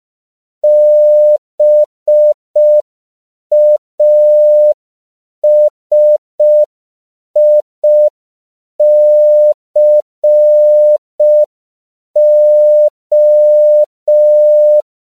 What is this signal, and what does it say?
Morse code, heard as a keyed tone, about 5 wpm, 600 Hz. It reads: BASICO
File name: CW / 5PPM